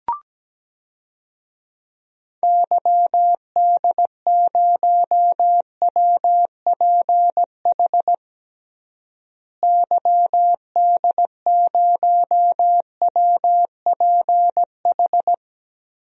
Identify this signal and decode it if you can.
Morse code, heard as a keyed tone, about 17 wpm, 700 Hz. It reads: YD0WPH YD0WPH
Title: DXCC-100 Call Signs - 6 Characters - Repeated 3x - 17wpm